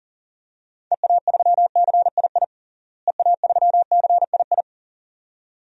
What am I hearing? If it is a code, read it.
Morse code, heard as a keyed tone, about 40 wpm, 700 Hz. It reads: EA3CII EA3CII
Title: DXCC-100 Call Signs - 6 Characters - Repeated 2x - 40wpm